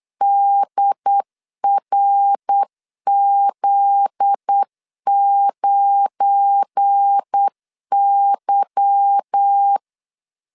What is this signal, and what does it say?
Morse code, heard as a keyed tone, about 8 wpm, 800 Hz. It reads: DRZ9Y
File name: LEZIONI / 8 YJ45 / 8 Y 8wpm 3245